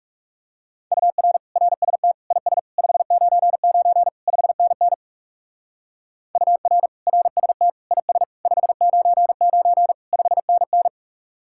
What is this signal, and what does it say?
Morse code, heard as a keyed tone, about 37 wpm, 700 Hz. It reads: UR RST IS 599 5NN UR RST IS 599 5NN
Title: Speed-Racing Common QSO Phrases - 30wpm